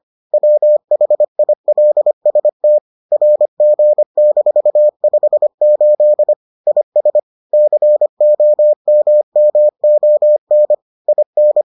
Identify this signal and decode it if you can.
Morse code, heard as a keyed tone, about 25 wpm, 600 Hz. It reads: WHILST RG-58 IS COMMON IN